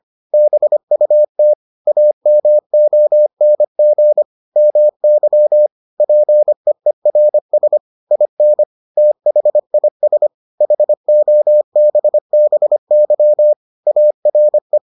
Morse code, heard as a keyed tone, about 25 wpm, 600 Hz. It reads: BUT AMONG MY PEERS IN THIS HOBBY ARE